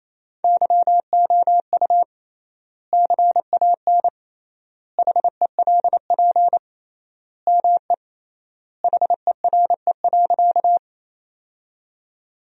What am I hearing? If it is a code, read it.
Morse code, heard as a keyed tone, about 28 wpm, 700 Hz. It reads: YOU CAN HELP ME HERE.
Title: 3x Word-Spacing - Sentences from Top 200 Words - 28wpm